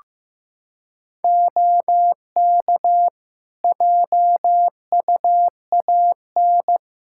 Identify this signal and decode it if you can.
Morse code, heard as a keyed tone, about 15 wpm, 700 Hz. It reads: OK JUAN
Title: Practice QSO - Exchange 4 - Ending - 15wpm